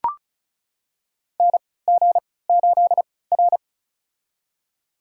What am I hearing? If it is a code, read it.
Morse code, heard as a keyed tone, about 36 wpm, spaced as for 20 wpm, 700 Hz. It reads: NG8R
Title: USA Call Signs - 3x Character Spacing - 35wpm